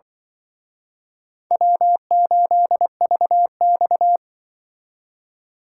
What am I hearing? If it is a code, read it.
Morse code, heard as a keyed tone, about 24 wpm, 700 Hz. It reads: W8VX
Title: Speed Racing - USA 1x2 Call Signs 20wpm